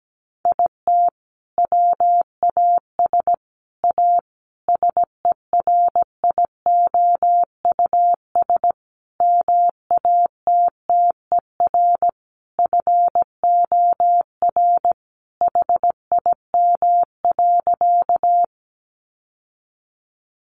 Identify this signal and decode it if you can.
Morse code, heard as a keyed tone, about 17 wpm, 700 Hz. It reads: IT WAS A SERIOUS MATTER FOR HIM.
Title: Sentences from Top 1100 Words 17wpm